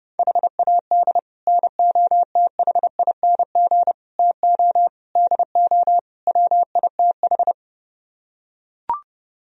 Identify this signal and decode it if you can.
Morse code, heard as a keyed tone, about 31 wpm, 700 Hz. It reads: HAD NOTHING TO DO WITH